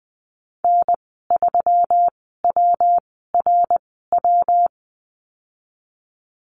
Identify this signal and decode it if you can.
Morse code, heard as a keyed tone, about 20 wpm, 700 Hz. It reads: N 3 W R W
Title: 154 Lesson 25g - Letter p - Call Signs 20wpm